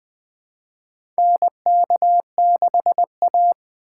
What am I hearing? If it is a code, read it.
Morse code, heard as a keyed tone, about 20 wpm, 700 Hz. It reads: NK6A
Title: 2023 SOTA Chaser Call Signs - 20wpm